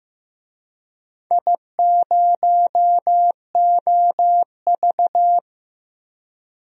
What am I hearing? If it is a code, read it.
Morse code, heard as a keyed tone, about 15 wpm, 700 Hz. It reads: I0OV
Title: DXCC-100 Call Signs - 4 Characters - Repeated 2x - 15wpm